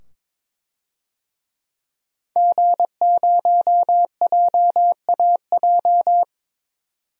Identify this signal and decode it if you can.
Morse code, heard as a keyed tone, about 22 wpm, 700 Hz. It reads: G0JAJ
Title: DXCC-100 Call Signs - 5 Characters - 22wpm